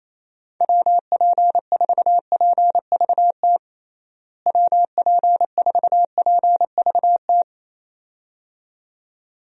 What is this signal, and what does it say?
Morse code, heard as a keyed tone, about 28 wpm, 700 Hz. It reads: WP4PVT WP4PVT
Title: USA Call Signs - 6 Characters - Repeated 2x - 28wpm